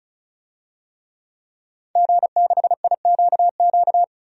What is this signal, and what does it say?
Morse code, heard as a keyed tone, about 36 wpm, 700 Hz. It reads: G6IQQ
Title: DXCC-100 Call Signs - 5 Characters - 35wpm